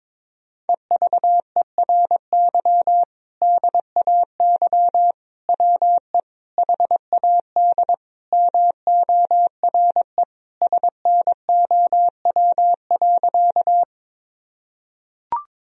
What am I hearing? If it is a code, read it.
Morse code, heard as a keyed tone, about 22 wpm, 700 Hz. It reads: EVERY DAY WE HAD MORE SNOW.
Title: Sentences from Top 1000 Words - Review Part 3 - 22wpm